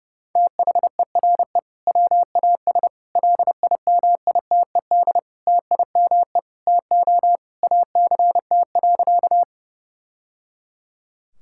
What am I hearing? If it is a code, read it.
Morse code, heard as a keyed tone, about 30 wpm, 700 Hz. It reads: THERE WAS LIMITED TIME TO ACT.
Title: Sentences from Top 1000 Words - Review Part 6 - 30wpm